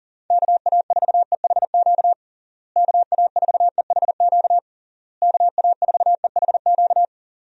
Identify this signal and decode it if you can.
Morse code, heard as a keyed tone, about 40 wpm, 700 Hz. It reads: KA4EHQ KA4EHQ KA4EHQ
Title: USA Call Signs - 6 Characters - Repeated 3x - 40wpm